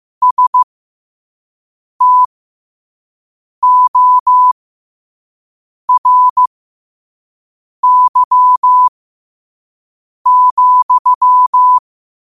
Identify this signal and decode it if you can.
Morse code, heard as a keyed tone, about 15 wpm, spaced as for 5 wpm, 1000 Hz. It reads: STORY,
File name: qotdfiles / 5wpm